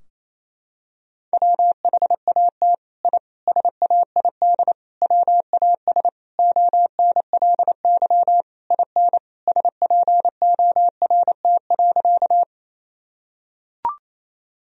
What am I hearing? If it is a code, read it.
Morse code, heard as a keyed tone, about 28 wpm, 700 Hz. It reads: WHAT I SAID WAS ONLY IN SPORT.
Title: Sentences from Top 1000 Words - Review Part 4 - 28wpm